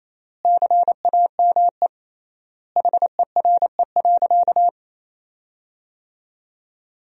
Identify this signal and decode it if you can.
Morse code, heard as a keyed tone, about 28 wpm, 700 Hz. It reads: CAME HERE.
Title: 3x Word-Spacing - Sentences from Top 500 Words - 28wpm